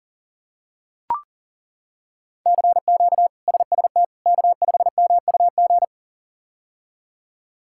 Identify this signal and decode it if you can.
Morse code, heard as a keyed tone, about 40 wpm, 700 Hz. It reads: CQ SST K5MUG
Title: K1USN SST QSO Practice - 40wpm